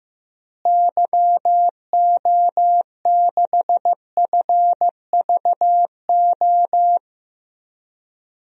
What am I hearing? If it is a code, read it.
Morse code, heard as a keyed tone, about 15 wpm, 700 Hz. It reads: YO6FVO
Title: DXCC-100 Call Signs - 6 Characters - Repeated 3x - 15wpm